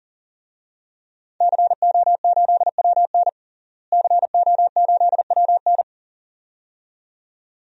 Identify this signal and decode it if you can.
Morse code, heard as a keyed tone, about 37 wpm, 700 Hz. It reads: CO8WN CO8WN
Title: DXCC-100 Call Signs - 5 Characters - Repeated 2x - 40wpm